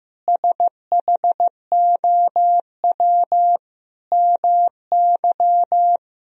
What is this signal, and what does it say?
Morse code, heard as a keyed tone, about 15 wpm, 700 Hz. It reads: SHOW MY
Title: Sentences from Top 1000 Words - Review Part 2 - 15wpm